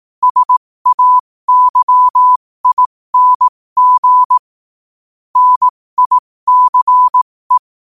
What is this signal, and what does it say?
Morse code, heard as a keyed tone, about 18 wpm, spaced as for 15 wpm, 1000 Hz. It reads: SAYING NICE